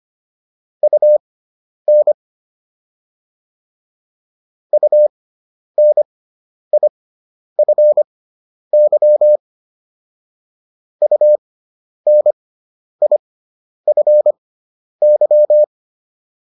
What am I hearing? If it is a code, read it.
Morse code, heard as a keyed tone, about 25 wpm, spaced as for 10 wpm, 600 Hz. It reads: UN UNIFY UNIFY